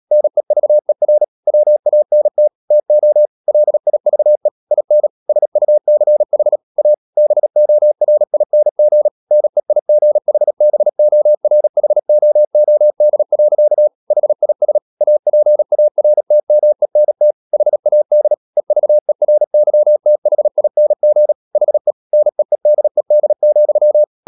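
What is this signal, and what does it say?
Morse code, heard as a keyed tone, about 37 wpm, 600 Hz. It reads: NEVER WANT TO LIVE IN SUCH A BORING NEIGHBORHOOD. HIS APARTMENT HAD EVERYTHING HE NEEDED,
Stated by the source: Morse code practice at 35–39 wpm sending English text of Ken Follett's novel Never, from CW med Gnister.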